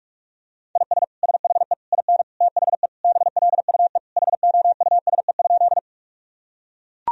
Morse code, heard as a keyed tone, about 41 wpm, 700 Hz. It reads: IS SHE IN THE BLUE HOUSE?
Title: Sentences from Top 1000 Words - Review Part 3 - 45wpm